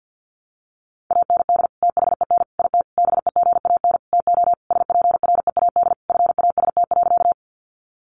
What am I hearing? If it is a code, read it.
Morse code, heard as a keyed tone, about 50 wpm, 700 Hz. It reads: AND THEN IT BEGAN TO SPREAD FAST.
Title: Sentences from Top 1200 Words 50wpm